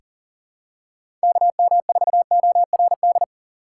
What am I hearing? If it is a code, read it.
Morse code, heard as a keyed tone, about 40 wpm, 700 Hz. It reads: KM4ORD
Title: USA Call Signs - 6 Characters - 3x Character Spacing - 40wpm